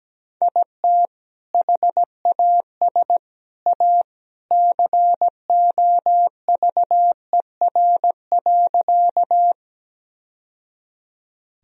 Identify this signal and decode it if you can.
Morse code, heard as a keyed tone, about 17 wpm, 700 Hz. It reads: IT HAS A COVER.
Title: Sentences from Top 1000 Words - Review Part 1 - 17wpm